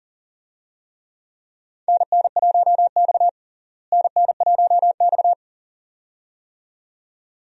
Morse code, heard as a keyed tone, about 40 wpm, 700 Hz. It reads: NN1X NN1X
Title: USA Call Signs - 4 Characters - Repeated 2x - 40wpm